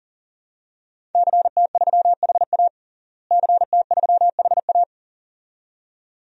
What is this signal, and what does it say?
Morse code, heard as a keyed tone, about 40 wpm, 700 Hz. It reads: CT3HA CT3HA
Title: DXCC-100 Call Signs - 5 Characters - Repeated 2x - 40wpm